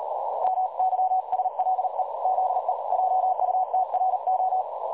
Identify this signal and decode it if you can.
Morse code, heard as a keyed tone, about 40 wpm, 750 Hz, with noise. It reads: MYSLE ZE JUTRO
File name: dlazaawansowanych / nagraniazpasma / websdr recording 3565.0kHz